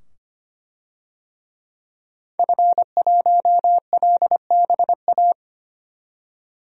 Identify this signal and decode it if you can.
Morse code, heard as a keyed tone, about 25 wpm, 700 Hz. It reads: F1LBA
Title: DXCC-100 Call Signs - 5 Characters - 25wpm